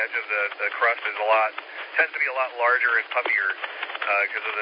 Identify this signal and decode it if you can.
Emergency Alert System (SAME) tones, heard as harsh buzzing bursts, with speech. It is heard from 0.6 to 3.6 s.